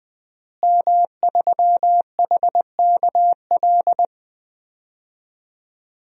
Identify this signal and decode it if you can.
Morse code, heard as a keyed tone, about 20 wpm, 700 Hz. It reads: M3HKL
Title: DXCC-100 Call Signs - 5 Characters - Repeated 2x - 20wpm